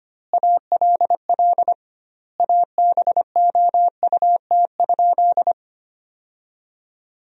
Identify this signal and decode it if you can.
Morse code, heard as a keyed tone, about 25 wpm, 700 Hz. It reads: ALL ABOUT?
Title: 2x Word-Spacing - Sentences from Top 100 Words - 25wpm